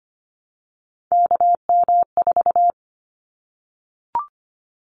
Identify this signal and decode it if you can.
Morse code, heard as a keyed tone, about 25 wpm, 700 Hz. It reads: KM4
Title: Callsign Prefixes for Top 25 DXCC - 25wpm